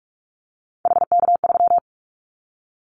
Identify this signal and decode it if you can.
Morse code, heard as a keyed tone, about 44 wpm, 700 Hz. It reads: HK3
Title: Callsign Prefixes for Top 100 DXCC - 45wpm